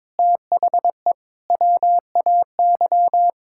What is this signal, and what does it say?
Morse code, heard as a keyed tone, about 22 wpm, 700 Hz. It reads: THE WAY
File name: Sentences with 7-Word Phrases - 22wpm